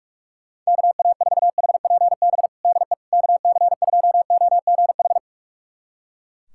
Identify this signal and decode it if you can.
Morse code, heard as a keyed tone, about 42 wpm, 700 Hz. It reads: KA4HPB DE KC2OGH
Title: USA Callsign DE Callsign 45wpm